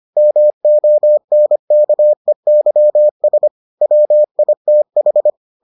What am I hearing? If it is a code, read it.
Morse code, heard as a keyed tone, about 25 wpm, 600 Hz. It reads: MONKEYS WITH